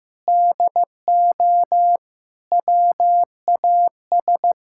Morse code, heard as a keyed tone, about 16 wpm, 700 Hz. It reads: DO WAS
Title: Sentences from Top 1000 Words - Review Part 6 - 15wpm